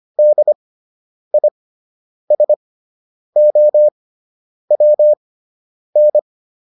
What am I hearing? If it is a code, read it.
Morse code, heard as a keyed tone, about 25 wpm, spaced as for 9 wpm, 600 Hz. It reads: DISOWN